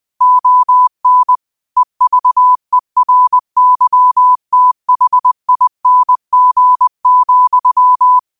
Morse code, heard as a keyed tone, about 20 wpm, 1000 Hz. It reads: ON EVERYTHING,
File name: qotdfiles / 20wpm